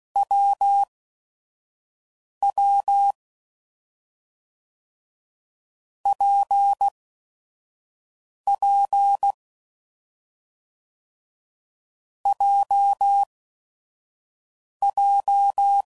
Morse code, heard as a keyed tone, about 16 wpm, spaced as for 5 wpm, 800 Hz. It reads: WW PP JJ
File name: Combinations / Difficult Combos / WPJ WPJ1